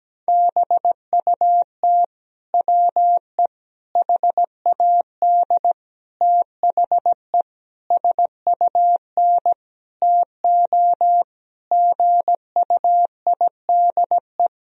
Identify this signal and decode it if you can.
Morse code, heard as a keyed tone, about 17 wpm, 700 Hz. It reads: BUT WE HAD THE SUN TO GUIDE
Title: Sentences from Top 1000 Words - Review Part 3 - 17wpm